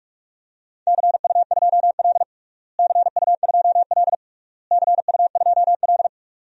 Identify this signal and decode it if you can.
Morse code, heard as a keyed tone, about 47 wpm, 700 Hz. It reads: CU2L CU2L CU2L
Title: DXCC-100 Call Signs - 4 Characters - Repeated 3x - 45wpm